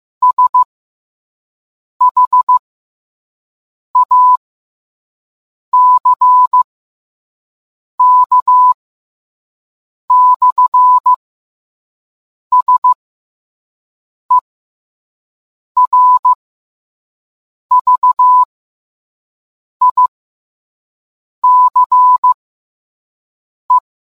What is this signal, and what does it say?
Morse code, heard as a keyed tone, about 15 wpm, spaced as for 5 wpm, 1000 Hz. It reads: SHACK/SERVICE